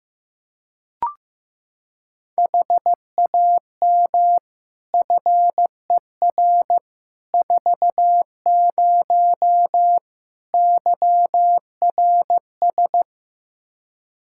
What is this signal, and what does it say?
Morse code, heard as a keyed tone, about 15 wpm, 700 Hz. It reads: HAM FER 40 YRS
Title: Practice QSO - Exchange 3 - More Info - 15wpm